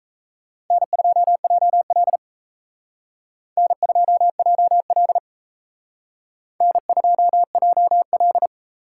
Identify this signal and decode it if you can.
Morse code, heard as a keyed tone, about 35 wpm, 700 Hz. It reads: N2JL N2JL N2JL
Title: Speed Racing - USA Call Signs 28wpm